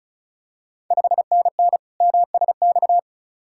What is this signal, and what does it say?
Morse code, heard as a keyed tone, about 36 wpm, 690 Hz. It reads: 5NN MSX
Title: Ontario QSO Party Exchange 35wpm